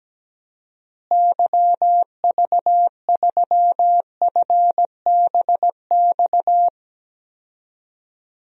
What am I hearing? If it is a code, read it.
Morse code, heard as a keyed tone, about 17 wpm, 700 Hz. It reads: YV3FBX
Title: DXCC-100 Call Signs - 6 Characters - Repeated 3x - 17wpm